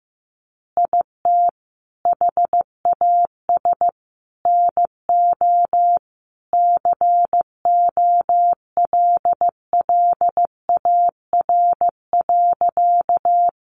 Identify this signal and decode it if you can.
Morse code, heard as a keyed tone, about 15 wpm, 700 Hz. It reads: IT HAS NO COLLAR.
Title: Easy Phrases 15wpm